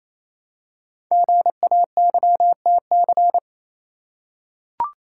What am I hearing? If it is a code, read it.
Morse code, heard as a keyed tone, about 29 wpm, 700 Hz. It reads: GAYTC
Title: Code Groups - 5 Letters - 28wpm